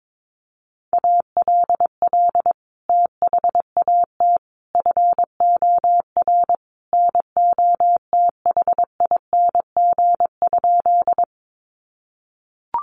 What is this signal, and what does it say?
Morse code, heard as a keyed tone, about 22 wpm, 700 Hz. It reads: ALL THAT FOR NOTHING?
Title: Sentences from Top 500 Words 22wpm